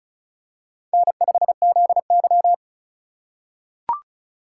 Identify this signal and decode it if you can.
Morse code, heard as a keyed tone, about 36 wpm, 700 Hz. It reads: N5ZY
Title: USA Call Signs - 4 Characters - 3x Character Spacing - 35wpm